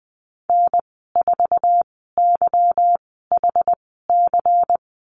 Morse code, heard as a keyed tone, about 20 wpm, spaced as for 14 wpm, 700 Hz. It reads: N4YHC
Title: 207 Lesson 33g - Letter j - Call Signs 20wpm